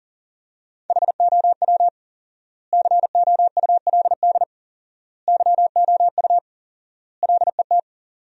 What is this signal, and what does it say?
Morse code, heard as a keyed tone, about 40 wpm, 700 Hz. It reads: HOW COULD YOU LET
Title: 4x Word-Spacing - Sentences from Top 300 Words - 40wpm